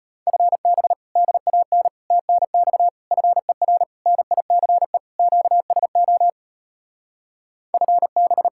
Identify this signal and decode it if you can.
Morse code, heard as a keyed tone, about 33 wpm, 700 Hz. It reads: FB DAN TNX FER NICE QSO FB